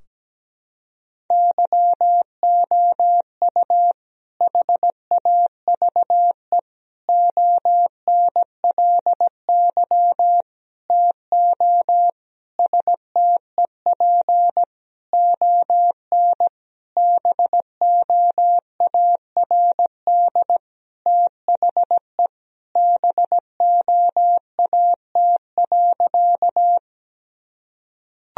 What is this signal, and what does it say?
Morse code, heard as a keyed tone, about 17 wpm, 700 Hz. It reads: YOU HAVE ONLY TO STEP ON BOARD THE BOAT.
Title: Sentences from Top 1000 Words - Review Part 4 - 17wpm